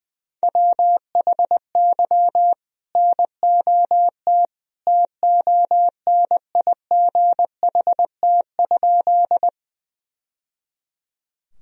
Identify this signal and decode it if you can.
Morse code, heard as a keyed tone, about 19 wpm, 700 Hz. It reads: WHY NOT TONIGHT?
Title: Sentences from Top 1000 Words - Review Part 3 - 20wpm